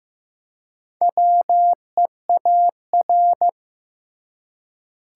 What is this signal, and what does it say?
Morse code, heard as a keyed tone, about 15 wpm, 700 Hz. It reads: WEAR
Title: Four Letter Words - 15wpm